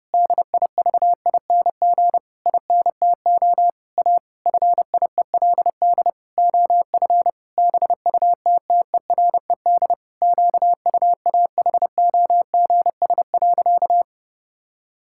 Morse code, heard as a keyed tone, about 30 wpm, 700 Hz. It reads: DIVING INTO A FIELD OF BUTTERED QUAHOGS.
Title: Pangram Challenge v2 - 30wpm